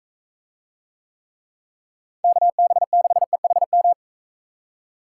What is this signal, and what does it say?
Morse code, heard as a keyed tone, about 41 wpm, 700 Hz. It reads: KB6EHM